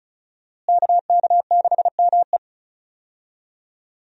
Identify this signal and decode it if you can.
Morse code, heard as a keyed tone, about 34 wpm, 700 Hz. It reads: KK6ME
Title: USA Call Signs - 5 Characters - Repeated 2x - 35wpm